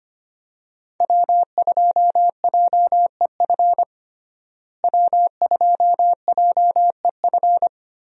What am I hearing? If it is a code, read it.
Morse code, heard as a keyed tone, about 25 wpm, 700 Hz. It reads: W2JEF W2JEF
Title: USA Call Signs - 5 Characters - Repeated 2x - 25wpm